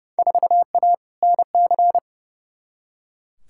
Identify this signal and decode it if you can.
Morse code, heard as a keyed tone, about 29 wpm, 700 Hz. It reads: 4A NC
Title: Field Day Exchanges - 30wpm